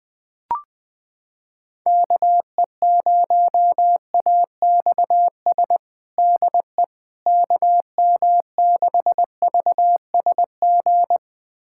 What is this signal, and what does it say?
Morse code, heard as a keyed tone, about 20 wpm, 700 Hz. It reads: KE0AXS DE KM6VSG
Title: USA Callsign DE Callsign 20wpm